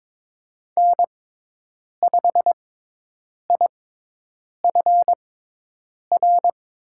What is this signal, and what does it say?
Morse code, heard as a keyed tone, about 22 wpm, spaced as for 8 wpm, 700 Hz. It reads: N5IFR
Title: USA Call Signs - 6x Character Spacing - 22wpm